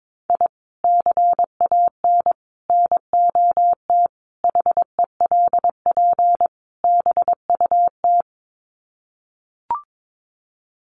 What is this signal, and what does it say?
Morse code, heard as a keyed tone, about 22 wpm, 700 Hz. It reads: I CAN NOT HELP BUT
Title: Sets of 5 Words Encore2 22wpm